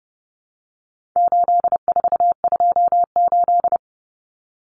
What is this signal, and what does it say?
Morse code, heard as a keyed tone, about 30 wpm, 700 Hz. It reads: 8428
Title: Numbers - 4-Digit 30wpm